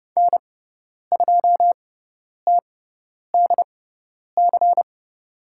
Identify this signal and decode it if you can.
Morse code, heard as a keyed tone, about 30 wpm, spaced as for 10 wpm, 700 Hz. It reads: N2TDC
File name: Lesson 16g - Letter c - Call Signs 30wpm